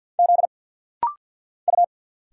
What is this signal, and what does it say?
Morse code, heard as a keyed tone, about 50 wpm, 700 Hz. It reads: Z U
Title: Single Letters - Rapid-Fire 50wpm